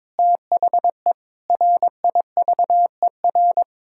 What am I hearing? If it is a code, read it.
Morse code, heard as a keyed tone, about 22 wpm, 700 Hz. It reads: THE RIVER
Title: Sentences from Top 1000 Words - Review Part 6 - 22wpm